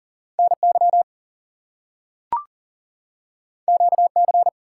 Morse code, heard as a keyed tone, about 40 wpm, 700 Hz. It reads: NY QC
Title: US States and Canadian Provinces 40wpm